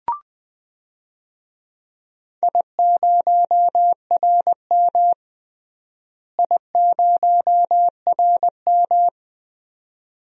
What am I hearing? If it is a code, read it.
Morse code, heard as a keyed tone, about 21 wpm, 700 Hz. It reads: I0RM I0RM
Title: DXCC-100 Call Signs - 4 Characters - Repeated 2x - 20wpm